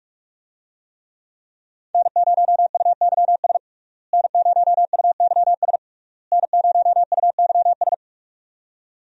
Morse code, heard as a keyed tone, about 43 wpm, 700 Hz. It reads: N0UYS N0UYS N0UYS
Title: USA Call Signs - 5 Characters - Repeated 3x - 45wpm